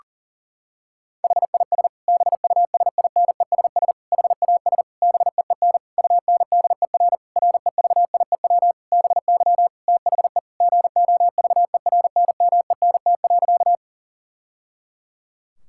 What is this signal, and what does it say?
Morse code, heard as a keyed tone, about 37 wpm, 700 Hz. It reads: HIS BUSINESS HAS BEEN UNDER REVIEW BY THE GOVERNMENT.
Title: Sentences from Top 1000 Words - Review Part 6 - 40wpm